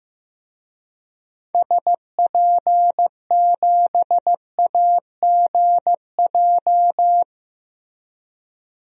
Morse code, heard as a keyed tone, about 15 wpm, 700 Hz. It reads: SP7AGJ
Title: DXCC-100 Call Signs - 6 Characters - Repeated 2x - 15wpm